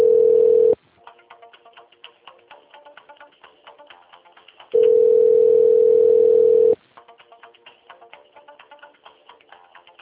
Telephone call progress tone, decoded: ringback tone

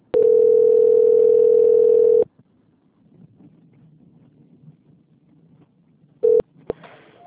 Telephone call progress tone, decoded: ringback tone

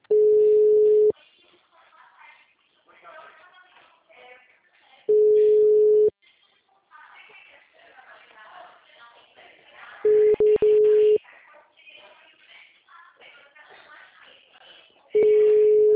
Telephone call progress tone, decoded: ringback tone

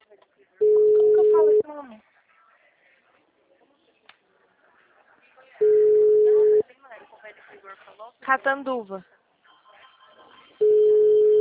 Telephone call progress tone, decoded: ringback tone

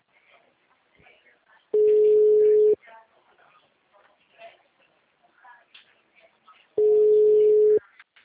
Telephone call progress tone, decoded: ringback tone

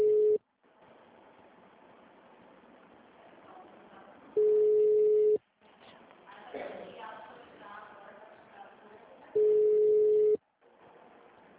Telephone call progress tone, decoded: ringback tone

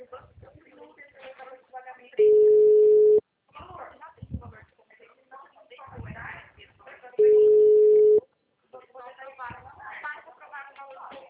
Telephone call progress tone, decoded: ringback tone

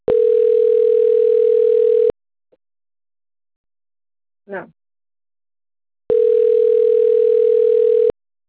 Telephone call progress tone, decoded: ringback tone

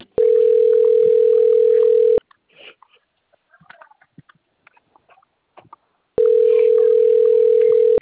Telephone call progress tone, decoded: ringback tone

